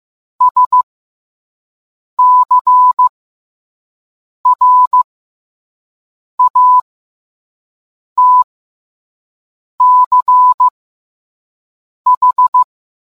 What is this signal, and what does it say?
Morse code, heard as a keyed tone, about 15 wpm, spaced as for 5 wpm, 1000 Hz. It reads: SCRATCH